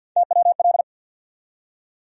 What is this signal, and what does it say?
Morse code, heard as a keyed tone, about 52 wpm, 700 Hz. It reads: TWL
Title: USA Call Sign Postfixes - 3 Letters 50wpm